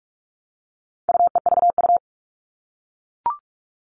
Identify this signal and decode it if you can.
Morse code, heard as a keyed tone, about 44 wpm, 700 Hz. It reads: UEVU
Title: All Cut Numbers - 4-Digit 45wpm